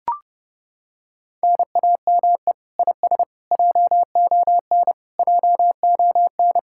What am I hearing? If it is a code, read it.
Morse code, heard as a keyed tone, about 29 wpm, 700 Hz. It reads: NAME IS JON JON